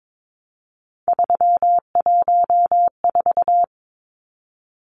Morse code, heard as a keyed tone, about 22 wpm, 700 Hz. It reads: 314
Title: Numbers - 3-Digit 22wpm